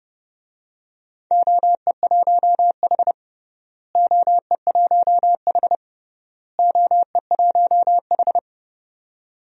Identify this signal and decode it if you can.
Morse code, heard as a keyed tone, about 31 wpm, 700 Hz. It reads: OE1H OE1H OE1H
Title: DXCC-100 Call Signs - 4 Characters - Repeated 3x - 30wpm